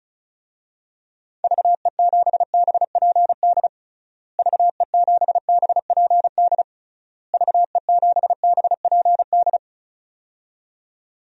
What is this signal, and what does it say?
Morse code, heard as a keyed tone, about 36 wpm, 700 Hz. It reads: VE7BPD VE7BPD VE7BPD
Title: DXCC-100 Call Signs - 6 Characters - Repeated 3x - 35wpm